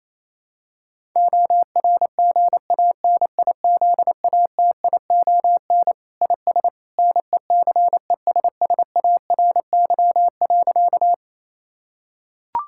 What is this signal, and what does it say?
Morse code, heard as a keyed tone, about 28 wpm, 700 Hz. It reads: ORGANIZATION IS NECESSARY.
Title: Sentences from Top 1000 Words - Review Part 5 - 28wpm